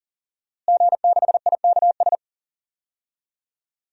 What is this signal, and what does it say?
Morse code, heard as a keyed tone, about 40 wpm, 700 Hz. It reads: G6IKS
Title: DXCC-100 Call Signs - 5 Characters - Repeated 2x - 40wpm